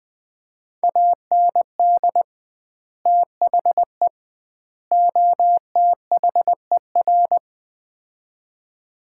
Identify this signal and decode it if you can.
Morse code, heard as a keyed tone, about 20 wpm, 700 Hz. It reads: AND THE OTHER
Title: 2x Word-Spacing - Sets of 3 Words - 20wpm